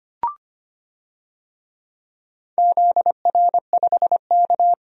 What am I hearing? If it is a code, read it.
Morse code, heard as a keyed tone, about 25 wpm, 700 Hz. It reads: ZR5K
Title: DXCC-100 Call Signs - 4 Characters - 25wpm